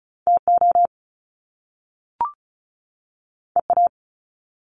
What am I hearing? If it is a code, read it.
Morse code, heard as a keyed tone, about 36 wpm, 700 Hz. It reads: TO EA
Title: Top Two Letter Combinations in Words 35wpm